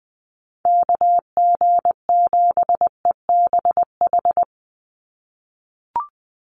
Morse code, heard as a keyed tone, about 20 wpm, 700 Hz. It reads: KG7EBH
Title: USA Call Signs 20wpm